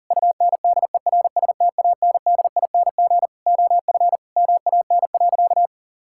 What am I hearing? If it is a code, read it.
Morse code, heard as a keyed tone, about 37 wpm, 700 Hz. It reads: UNDERSTANDING OF MAN.
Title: Sentences from Top 1000 Words - Review Part 5 - 40wpm